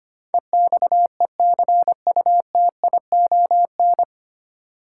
Morse code, heard as a keyed tone, about 25 wpm, 700 Hz. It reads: EXECUTION
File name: Nine Letter Words - 25wpm